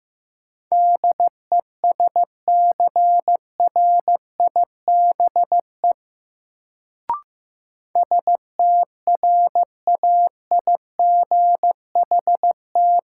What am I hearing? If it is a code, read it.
Morse code, heard as a keyed tone, about 15 wpm, 700 Hz. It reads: DESCRIBE STRAIGHT
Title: Sentences from Top 1000 Words - New Word Drill - 15wpm